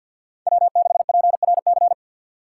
Morse code, heard as a keyed tone, about 48 wpm, 700 Hz. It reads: W6PRC
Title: USA Call Signs - 5 Characters - 2x Character Spacing - 50wpm